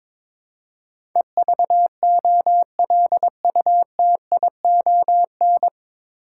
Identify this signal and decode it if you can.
Morse code, heard as a keyed tone, about 21 wpm, 700 Hz. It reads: EVOLUTION
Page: Morse Code Ninja practice 22wpm